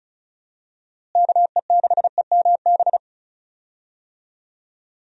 Morse code, heard as a keyed tone, about 36 wpm, 700 Hz. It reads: KE6EMB
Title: USA Call Signs - 6 Characters - Repeated 2x - 35wpm